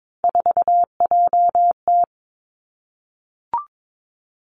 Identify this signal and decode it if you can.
Morse code, heard as a keyed tone, about 22 wpm, 700 Hz. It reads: 4JT
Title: Three Challenging Letter-Number Combinations 22wpm